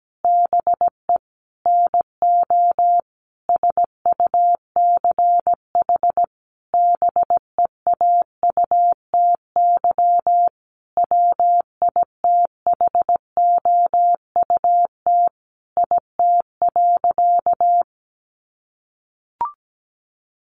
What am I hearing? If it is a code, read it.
Morse code, heard as a keyed tone, about 17 wpm, 700 Hz. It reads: BE NO SUCH BEAUTY WITHOUT IT.
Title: Sentences from Top 600 Words 17wpm